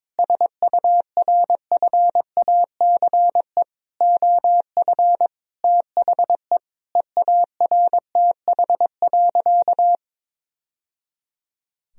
Morse code, heard as a keyed tone, about 22 wpm, 700 Hz. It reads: SURFACE OF THE EARTH.
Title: Sentences from Top 1000 Words - Review Part 3 - 22wpm